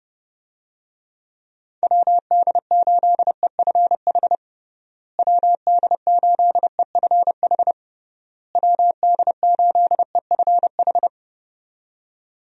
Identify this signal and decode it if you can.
Morse code, heard as a keyed tone, about 30 wpm, 700 Hz. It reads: WD8EFH WD8EFH WD8EFH
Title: USA Call Signs - 6 Characters - Repeated 3x - 30wpm